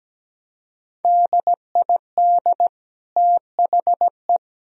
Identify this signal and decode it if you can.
Morse code, heard as a keyed tone, about 17 wpm, 700 Hz. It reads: DID THE
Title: Sentences from Top 1000 Words - Review Part 4 - 17wpm